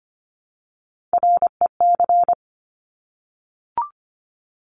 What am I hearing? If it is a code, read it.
Morse code, heard as a keyed tone, about 25 wpm, 700 Hz. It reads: REC
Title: Top Three Letter Combinations in Words 25wpm